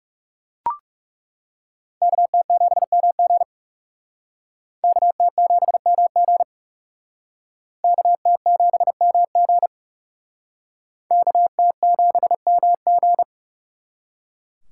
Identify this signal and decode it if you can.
Morse code, heard as a keyed tone, about 37 wpm, 700 Hz. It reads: KT7MG KT7MG KT7MG KT7MG
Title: Speed Racing - USA 2x2 Call Signs 30wpm